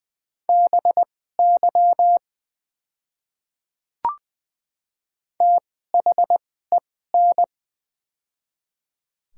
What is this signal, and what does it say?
Morse code, heard as a keyed tone, about 21 wpm, spaced as for 15 wpm, 700 Hz. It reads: BY THEN
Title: 2x Character Spacing - Top 100 Words - 20wpm